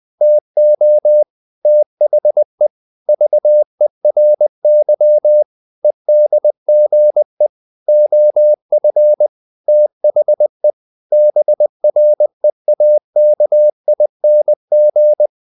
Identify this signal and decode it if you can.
Morse code, heard as a keyed tone, about 20 wpm, 600 Hz. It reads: TO THE VERY EDGE OF THE BREAKING
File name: the old man and the sea / the old man and the sea part 1 20wpm